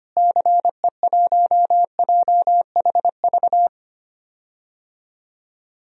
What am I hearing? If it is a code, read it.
Morse code, heard as a keyed tone, about 25 wpm, 700 Hz. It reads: CE1JHV
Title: DXCC-100 Call Signs - 6 Characters - Repeated 3x - 25wpm